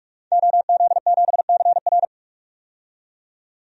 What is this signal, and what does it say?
Morse code, heard as a keyed tone, about 43 wpm, 700 Hz. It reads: OZ7CR
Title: DXCC-100 Call Signs - 5 Characters - 45wpm